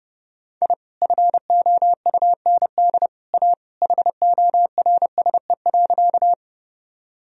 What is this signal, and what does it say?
Morse code, heard as a keyed tone, about 30 wpm, 700 Hz. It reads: I FOUND A HORSE.
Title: Sentences from Top 1000 Words - Review Part 1 - 30wpm